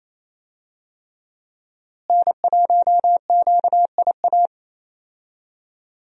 Morse code, heard as a keyed tone, about 28 wpm, 700 Hz. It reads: N1QIA
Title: USA Call Signs - 5 Characters - Repeated 3x - 28wpm